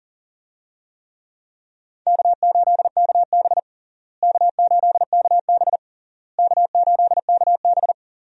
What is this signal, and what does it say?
Morse code, heard as a keyed tone, about 40 wpm, 700 Hz. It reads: K8KB K8KB K8KB
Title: USA Call Signs - 4 Characters - Repeated 3x - 40wpm